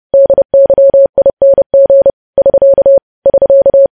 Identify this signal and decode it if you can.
Morse code, heard as a keyed tone, about 30 wpm, 550 Hz. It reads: DYING <SK> <SK>
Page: news headlines in morse code at 30 wpm